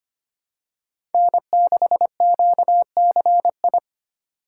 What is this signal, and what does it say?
Morse code, heard as a keyed tone, about 25 wpm, 700 Hz. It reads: N6QCI
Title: USA Call Signs - 5 Characters - 4x Character Spacing - 25wpm